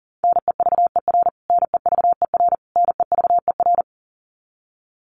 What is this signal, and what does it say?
Morse code, heard as a keyed tone, about 40 wpm, 700 Hz. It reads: NEVER NEVER NEVER
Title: Top 500 Words Repeated 40wpm